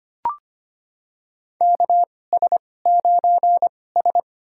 Morse code, heard as a keyed tone, about 25 wpm, spaced as for 18 wpm, 700 Hz. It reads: KS9S
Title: USA Call Signs - 4 Characters - 2x Character Spacing - 25wpm